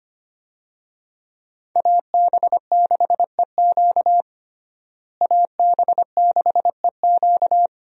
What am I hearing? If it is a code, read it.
Morse code, heard as a keyed tone, about 25 wpm, 700 Hz. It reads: AB6EQ AB6EQ
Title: USA Call Signs - 5 Characters - Repeated 3x - 25wpm